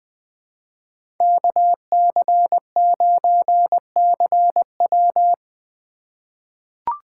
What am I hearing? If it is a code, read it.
Morse code, heard as a keyed tone, about 20 wpm, 700 Hz. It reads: KC9CW
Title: USA Call Signs - 5 Characters - 2x Character Spacing - 20wpm